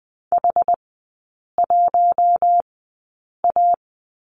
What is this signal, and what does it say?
Morse code, heard as a keyed tone, about 20 wpm, spaced as for 8 wpm, 700 Hz. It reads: H1A
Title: 093 Lesson 16e - Letter c - Three Characters 20wpm